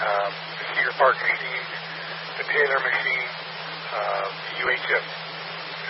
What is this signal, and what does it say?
Emergency Alert System (SAME) tones, heard as harsh buzzing bursts, with speech. It is heard from 1.2 to 5.0 s.